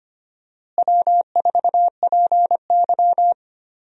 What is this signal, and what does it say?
Morse code, heard as a keyed tone, about 25 wpm, 700 Hz. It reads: W4PY
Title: USA Call Signs - 4 Characters - Repeated 3x - 25wpm